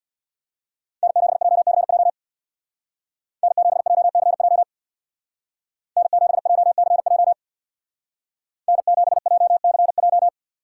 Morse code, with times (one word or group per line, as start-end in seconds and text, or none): 1.0-2.1 N7JCJ
3.4-4.6 N7JCJ
6.0-7.3 N7JCJ
8.7-10.3 N7JCJ